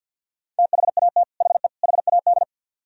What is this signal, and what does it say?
Morse code, heard as a keyed tone, about 60 wpm, 700 Hz. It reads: THAT HE HAD
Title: Speed-Racing - Sets of 3 Words 50wpm